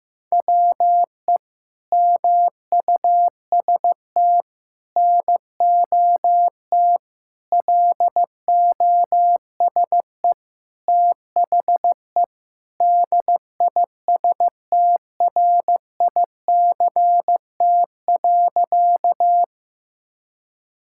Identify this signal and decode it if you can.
Morse code, heard as a keyed tone, about 15 wpm, 700 Hz. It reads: WE MUST NOT LOSE THE DISTRICT.
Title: Sentences from Top 1600 Words 15wpm